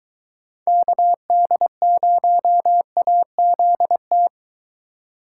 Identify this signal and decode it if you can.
Morse code, heard as a keyed tone, about 23 wpm, 700 Hz. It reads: KD0AZT